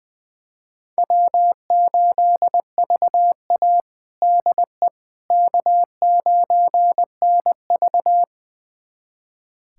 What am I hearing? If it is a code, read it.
Morse code, heard as a keyed tone, about 20 wpm, 700 Hz. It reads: W8VA DE K9NV
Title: USA Callsign DE Callsign - 4 Characters - 20wpm